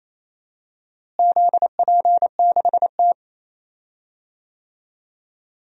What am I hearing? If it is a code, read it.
Morse code, heard as a keyed tone, about 28 wpm, 700 Hz. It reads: ZP6T
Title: DXCC-100 Call Signs - 4 Characters - Repeated 3x - 28wpm